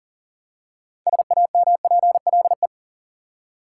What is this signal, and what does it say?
Morse code, heard as a keyed tone, about 40 wpm, 700 Hz. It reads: SAMPLE
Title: Six Letter Words - 40wpm